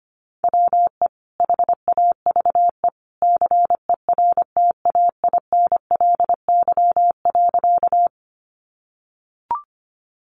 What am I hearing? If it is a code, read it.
Morse code, heard as a keyed tone, about 25 wpm, 700 Hz. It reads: WE HAVE CERTAINLY.
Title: Sentences from Top 900 Words 25wpm